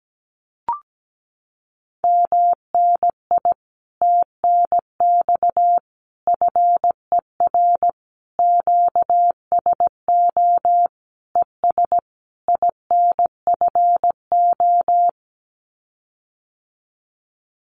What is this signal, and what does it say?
Morse code, heard as a keyed tone, about 17 wpm, 700 Hz. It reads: MNI TNX FER QSO ES INFO
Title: Common QSO Phrases Encore 17wpm